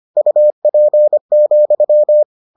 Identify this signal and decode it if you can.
Morse code, heard as a keyed tone, about 25 wpm, 600 Hz. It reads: UP,